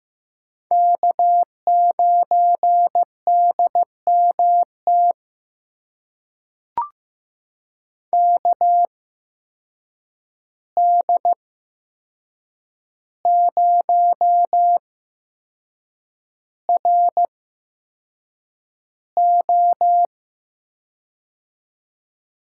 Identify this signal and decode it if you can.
Morse code, heard as a keyed tone, about 15 wpm, spaced as for 6 wpm, 700 Hz. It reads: K9DMT KD0RO
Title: USA Call Signs - 5 Characters - 8x Character Spacing - 15wpm